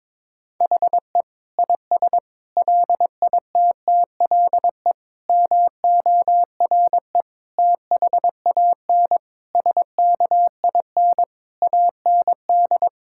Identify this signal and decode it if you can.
Morse code, heard as a keyed tone, about 21 wpm, 700 Hz. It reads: HE IS LITTLE MORE THAN SKIN AND
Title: Sentences from Top 1600 Words 22wpm